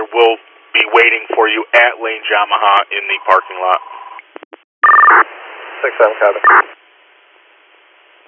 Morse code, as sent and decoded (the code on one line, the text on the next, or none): none